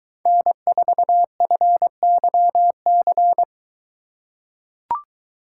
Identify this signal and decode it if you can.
Morse code, heard as a keyed tone, about 21 wpm, 700 Hz. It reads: N4FYC